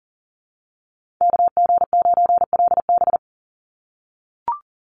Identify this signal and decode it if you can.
Morse code, heard as a keyed tone, about 41 wpm, 700 Hz. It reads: KG9LB
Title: Call Signs - 40wpm